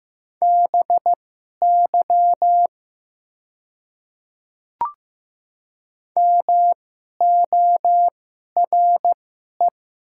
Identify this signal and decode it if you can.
Morse code, heard as a keyed tone, about 15 wpm, spaced as for 11 wpm, 700 Hz. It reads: BY MORE
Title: 2x Character Spacing - Top 100 Words - 15wpm